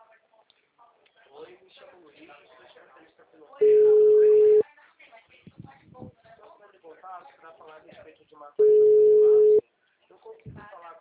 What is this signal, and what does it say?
Ringback tone (Europe, 425 Hz), with speech.